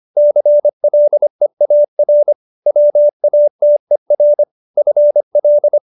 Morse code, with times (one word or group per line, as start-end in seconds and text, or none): 0.2-2.3 CLEAR
2.7-4.4 WATER
4.8-5.8 FL